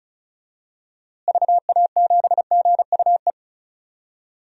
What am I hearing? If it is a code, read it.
Morse code, heard as a keyed tone, about 36 wpm, 700 Hz. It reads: VA7GUE
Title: DXCC-100 Call Signs - 6 Characters - Repeated 3x - 35wpm